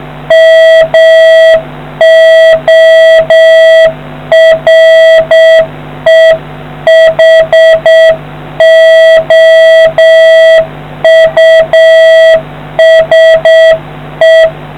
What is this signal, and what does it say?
Morse code, heard as a keyed tone, about 7 wpm, 650 Hz. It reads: MOREHOUSE